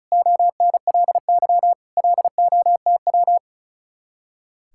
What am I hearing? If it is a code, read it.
Morse code, heard as a keyed tone, about 43 wpm, 700 Hz. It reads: ONLY LOTW
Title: Speed-Racing Common QSO Phrases - 35wpm